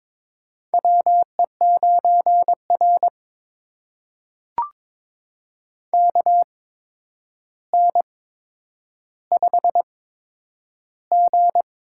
Morse code, heard as a keyed tone, about 22 wpm, spaced as for 9 wpm, 700 Hz. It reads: WE9R KN5G
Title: USA Call Signs - 4 Characters - 8x Character Spacing - 22wpm